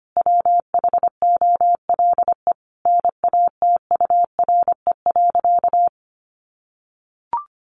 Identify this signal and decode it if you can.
Morse code, heard as a keyed tone, about 25 wpm, 700 Hz. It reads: WHOLE NATURE.
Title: Sentences from Top 1500 Words 25wpm